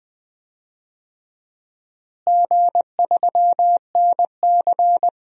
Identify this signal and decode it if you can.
Morse code, heard as a keyed tone, about 20 wpm, 700 Hz. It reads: G3NC
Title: DXCC-100 Call Signs - 4 Characters - Repeated 3x - 20wpm